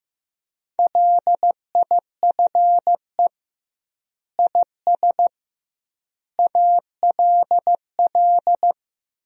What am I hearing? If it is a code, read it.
Morse code, heard as a keyed tone, about 15 wpm, 700 Hz. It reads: LIFE IS ALL
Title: 2x Word-Spacing - Sentences from Top 300 Words - 15wpm